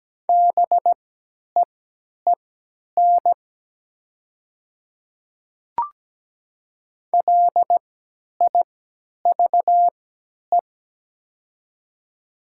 Morse code, heard as a keyed tone, about 17 wpm, spaced as for 10 wpm, 700 Hz. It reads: BEEN LIVE
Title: 3x Character Spacing - Top 500 Words - 17wpm